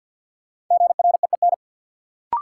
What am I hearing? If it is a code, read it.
Morse code, heard as a keyed tone, about 53 wpm, 700 Hz. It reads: GREEN
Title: Sentences from Top 500 Words - New Word Drill - 50wpm